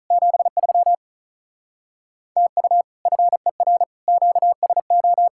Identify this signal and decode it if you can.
Morse code, heard as a keyed tone, about 45 wpm, 700 Hz. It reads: 73 TU FER QSO